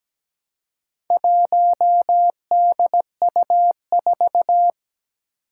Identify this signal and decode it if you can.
Morse code, heard as a keyed tone, about 17 wpm, 700 Hz. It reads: 1DU4
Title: Four Challenging Letter-Number Combinations 17wpm